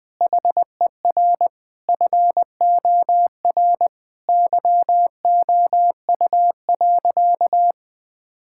Morse code, heard as a keyed tone, about 20 wpm, 700 Hz. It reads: HER FOR YOU.
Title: Sentences from Top 1000 Words - Review Part 6 - 20wpm